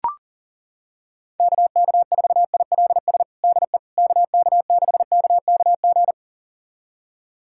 Morse code, heard as a keyed tone, about 40 wpm, 700 Hz. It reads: KK4ILS DE KK6KKG
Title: USA Callsign DE Callsign - 6 Characters - 40wpm